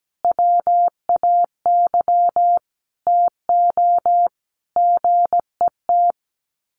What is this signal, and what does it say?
Morse code, heard as a keyed tone, about 17 wpm, 700 Hz. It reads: WAY TO GET
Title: Sets of 5 Words Encore2 17wpm